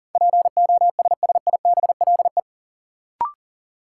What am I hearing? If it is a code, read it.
Morse code, heard as a keyed tone, about 40 wpm, 700 Hz. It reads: POSSIBLE
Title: Sentences from Top 500 Words - New Word Drill - 40wpm